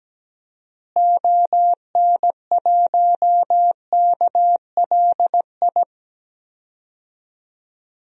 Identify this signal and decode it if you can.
Morse code, heard as a keyed tone, about 17 wpm, 700 Hz. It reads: ON1KLI
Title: DXCC-100 Call Signs - 6 Characters - Repeated 3x - 17wpm